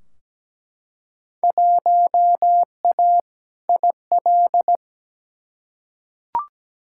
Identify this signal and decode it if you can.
Morse code, heard as a keyed tone, about 17 wpm, 700 Hz. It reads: 1A IL
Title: Field Day Exchanges - 17wpm